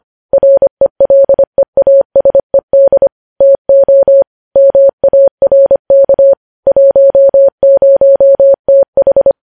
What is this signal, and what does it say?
Morse code, heard as a keyed tone, about 25 wpm, 550 Hz. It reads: RELEASED TO MARK 10TH
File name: podcast / News headlines-1x-25x25